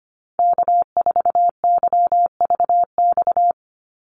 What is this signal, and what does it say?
Morse code, heard as a keyed tone, about 25 wpm, 700 Hz. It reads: K4YVX
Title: USA Call Signs 25wpm